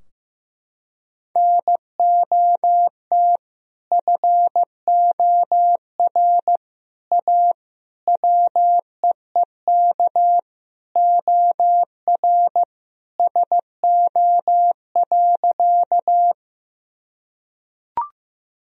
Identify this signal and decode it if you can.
Morse code, heard as a keyed tone, about 15 wpm, 700 Hz. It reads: NOT FOR A WEEK OR SO.
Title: Sentences from Top 1000 Words - Review Part 2 - 15wpm